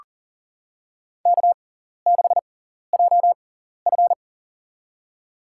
Morse code, heard as a keyed tone, about 43 wpm, spaced as for 14 wpm, 700 Hz. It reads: K6JF
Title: USA Call Signs - 4 Characters - 6x Character Spacing - 40wpm